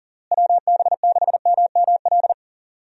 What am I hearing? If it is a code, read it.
Morse code, heard as a keyed tone, about 40 wpm, 700 Hz. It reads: WB6MML
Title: USA Call Signs - 6 Characters - Repeated 3x - 40wpm